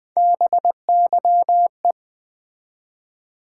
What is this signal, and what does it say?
Morse code, heard as a keyed tone, about 20 wpm, 700 Hz. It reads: BYE
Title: Build A Word - 1 Letter At A Time - 3-Letter Words - 20wpm